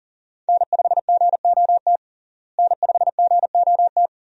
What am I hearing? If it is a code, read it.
Morse code, heard as a keyed tone, about 40 wpm, 700 Hz. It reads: N5GOT N5GOT
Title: USA Call Signs - 5 Characters - Repeated 2x - 40wpm